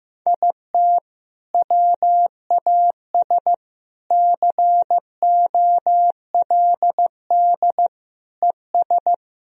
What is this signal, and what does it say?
Morse code, heard as a keyed tone, about 16 wpm, 700 Hz. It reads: IT WAS COLD ES
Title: Sentences from Top 500 Words and Abbreviations 15wpm